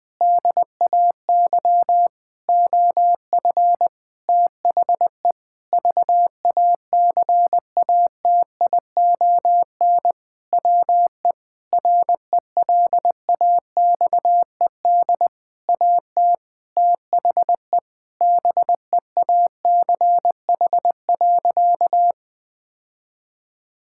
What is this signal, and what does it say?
Morse code, heard as a keyed tone, about 20 wpm, 700 Hz. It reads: DAY OF THE VACATION WE RELAXED AT THE BEACH.
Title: Sentences with 6-Word Phrases - 20wpm